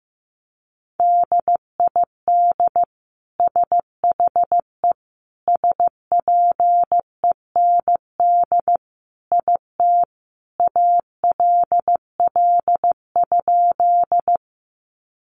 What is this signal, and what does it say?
Morse code, heard as a keyed tone, about 15 wpm, 700 Hz. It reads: DID SHE SPEND IT ALL?
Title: Sentences from Top 800 Words 15wpm